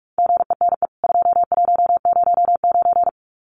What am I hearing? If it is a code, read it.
Morse code, heard as a keyed tone, about 44 wpm, 700 Hz. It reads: GENE 2109
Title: CWT Exchange 45wpm